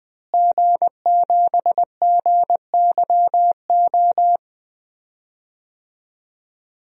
Morse code, heard as a keyed tone, about 20 wpm, 700 Hz. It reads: G7GYO
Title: DXCC-100 Call Signs - 5 Characters - Repeated 2x - 20wpm